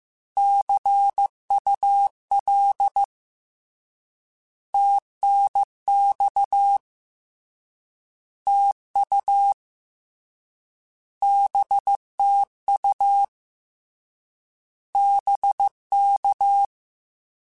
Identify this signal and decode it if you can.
Morse code, heard as a keyed tone, about 15 wpm, 800 Hz. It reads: CUL TNX TU BTU BK
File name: Short Runs / Short Runs-mp3s / Words @13